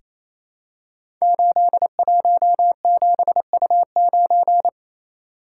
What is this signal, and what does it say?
Morse code, heard as a keyed tone, about 29 wpm, 700 Hz. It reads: 817U9
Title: Code Groups - 5 Letter:Numbers - 28wpm